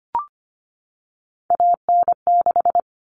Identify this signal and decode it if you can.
Morse code, heard as a keyed tone, about 25 wpm, 700 Hz. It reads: AN6
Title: All Cut Numbers - 3-Digit 25wpm